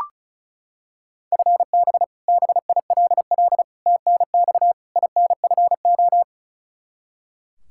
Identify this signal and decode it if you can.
Morse code, heard as a keyed tone, about 34 wpm, 700 Hz. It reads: FB BILL TNX INFO
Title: Practice QSO - Exchange 3 - More Info - 35wpm